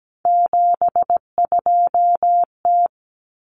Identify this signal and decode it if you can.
Morse code, heard as a keyed tone, about 17 wpm, 700 Hz. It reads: 72T
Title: Common Cut Numbers - 3-Digit 17wpm